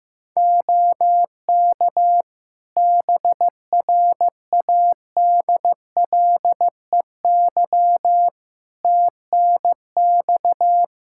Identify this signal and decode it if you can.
Morse code, heard as a keyed tone, about 15 wpm, 700 Hz. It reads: OK BRADLEY TNX
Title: Practice QSO - Exchange 4 - Ending - 15wpm